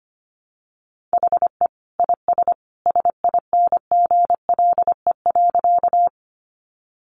Morse code, heard as a keyed tone, about 25 wpm, 700 Hz. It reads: HE IS SINGLE.
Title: Sentences from Top 700 Words 25wpm